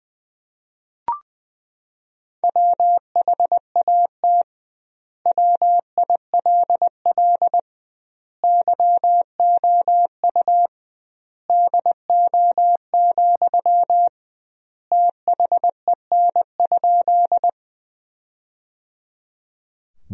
Morse code, heard as a keyed tone, about 20 wpm, 700 Hz. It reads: WHAT WILL YOU DO, THEN?
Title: 2x Word-Spacing - Sentences from Top 100 Words - 20wpm